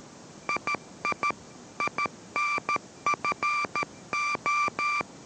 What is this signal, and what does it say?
Morse code, heard as a keyed tone, about 13 wpm, 1180 Hz, with noise. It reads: II INFO